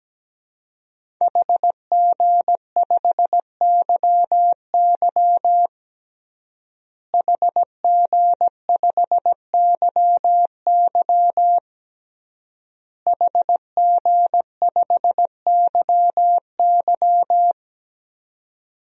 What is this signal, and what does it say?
Morse code, heard as a keyed tone, about 17 wpm, 700 Hz. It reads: HG5YY HG5YY HG5YY
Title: DXCC-100 Call Signs - 5 Characters - Repeated 3x - 17wpm